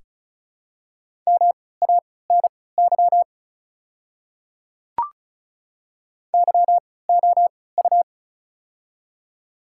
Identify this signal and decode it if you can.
Morse code, heard as a keyed tone, about 36 wpm, spaced as for 20 wpm, 690 Hz. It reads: MANY YOU
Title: 3x Character Spacing - Top 100 Words - 35wpm